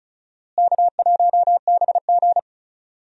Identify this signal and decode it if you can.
Morse code, heard as a keyed tone, about 34 wpm, 700 Hz. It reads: K1BG
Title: USA Call Signs - 4 Characters - Repeated 2x - 35wpm